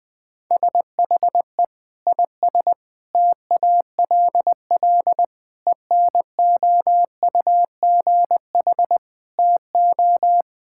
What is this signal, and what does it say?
Morse code, heard as a keyed tone, about 21 wpm, 700 Hz. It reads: SHE IS TALL ENOUGH TO
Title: Sentences with 2-Word Phrases - 20wpm